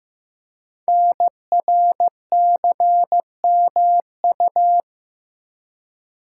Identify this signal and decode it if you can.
Morse code, heard as a keyed tone, about 15 wpm, 700 Hz. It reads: NRCMU
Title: Code Groups - 5 Letters - 15wpm